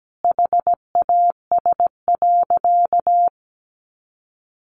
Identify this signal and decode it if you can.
Morse code, heard as a keyed tone, about 17 wpm, 700 Hz. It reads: HAS.
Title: Sentences from Top 500 Words 17wpm